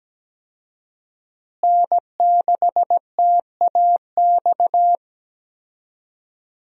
Morse code, heard as a keyed tone, about 17 wpm, 700 Hz. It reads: N6TAX
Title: USA Call Signs - 5 Characters - Repeated 3x - 17wpm